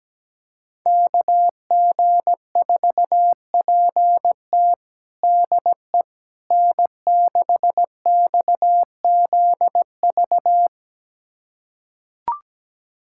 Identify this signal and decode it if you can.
Morse code, heard as a keyed tone, about 17 wpm, 700 Hz. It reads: KG4PT DE N6XZV
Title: USA Callsign DE Callsign - 5 Characters - 17wpm